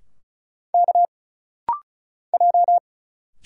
Morse code, heard as a keyed tone, about 35 wpm, 700 Hz. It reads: K J
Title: Single Letters - Rapid-Fire 35wpm